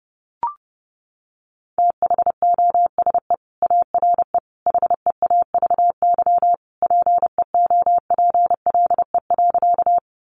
Morse code, heard as a keyed tone, about 30 wpm, 700 Hz. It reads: THOSE ARE HEAVY PEOPLE.
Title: Sentences from Top 1100 Words 30wpm